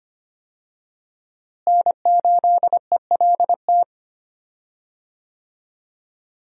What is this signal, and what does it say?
Morse code, heard as a keyed tone, about 25 wpm, 700 Hz. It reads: N8ELT
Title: USA Call Signs - 5 Characters - Repeated 3x - 25wpm